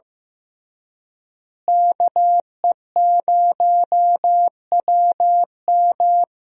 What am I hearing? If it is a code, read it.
Morse code, heard as a keyed tone, about 16 wpm, 700 Hz. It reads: KE0WM
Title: USA Call Signs - 5 Characters - Repeated 3x - 15wpm